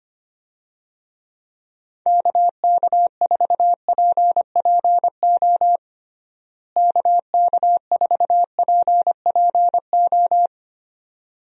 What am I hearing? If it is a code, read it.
Morse code, heard as a keyed tone, about 25 wpm, 700 Hz. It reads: KK4PPO KK4PPO
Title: USA Call Signs - 6 Characters - Repeated 2x - 25wpm